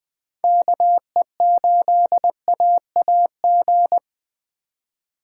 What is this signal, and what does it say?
Morse code, heard as a keyed tone, about 20 wpm, 700 Hz. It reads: KE8AAG
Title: Speed Racing - USA Call Signs 17wpm